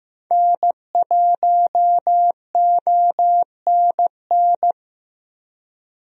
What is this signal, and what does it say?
Morse code, heard as a keyed tone, about 15 wpm, 700 Hz. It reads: N1ONN
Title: USA Call Signs - 5 Characters - 15wpm